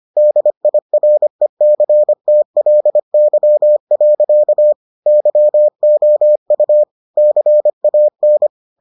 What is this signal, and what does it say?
Morse code, heard as a keyed tone, about 25 wpm, 600 Hz. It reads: DIRECTLY. YOU CAN